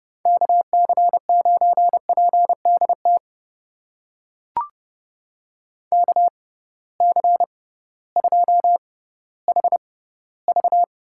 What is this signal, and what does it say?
Morse code, heard as a keyed tone, about 30 wpm, spaced as for 16 wpm, 700 Hz. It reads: KC9PDT KC2HV
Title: USA Call Signs - 6 Characters - 6x Character Spacing - 30wpm